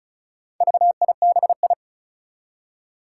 Morse code, heard as a keyed tone, about 36 wpm, 700 Hz. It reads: VIBI
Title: Four Challenging Letter-Number Combinations 35wpm